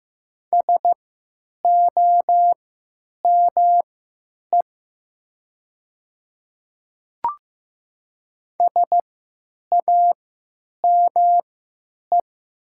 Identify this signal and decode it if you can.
Morse code, heard as a keyed tone, about 15 wpm, spaced as for 9 wpm, 700 Hz. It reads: SOME SAME
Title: 3x Character Spacing - Top 300 Words - 15wpm